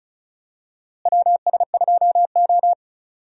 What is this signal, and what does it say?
Morse code, heard as a keyed tone, about 36 wpm, 700 Hz. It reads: WS2O
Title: USA Call Signs - 4 Characters - Repeated 3x - 35wpm